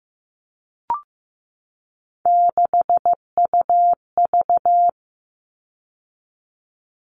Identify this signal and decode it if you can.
Morse code, heard as a keyed tone, about 15 wpm, 700 Hz. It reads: 6UV
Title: All Cut Numbers - 3-Digit 15wpm